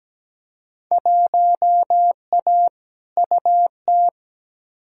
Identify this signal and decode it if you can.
Morse code, heard as a keyed tone, about 17 wpm, 700 Hz. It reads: 1A UT
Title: Field Day Exchanges - 17wpm